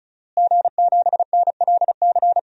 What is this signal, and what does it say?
Morse code, heard as a keyed tone, about 36 wpm, 700 Hz. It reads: G7NLC
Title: DXCC-100 Call Signs - 5 Characters - Repeated 2x - 35wpm